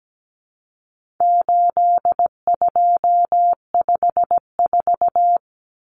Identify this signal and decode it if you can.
Morse code, heard as a keyed tone, about 17 wpm, 700 Hz. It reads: 8254
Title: Common Cut Numbers - 4-Digit 17wpm